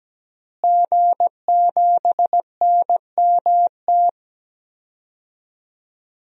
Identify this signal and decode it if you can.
Morse code, heard as a keyed tone, about 17 wpm, 700 Hz. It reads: G7NMT
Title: DXCC-100 Call Signs - 5 Characters - Repeated 3x - 17wpm